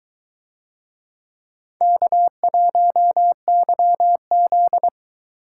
Morse code, heard as a keyed tone, about 22 wpm, 700 Hz. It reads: K1YZ